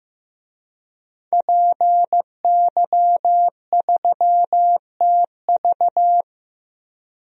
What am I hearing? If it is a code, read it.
Morse code, heard as a keyed tone, about 16 wpm, 700 Hz. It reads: PY3TV
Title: DXCC-100 Call Signs - 5 Characters - 15wpm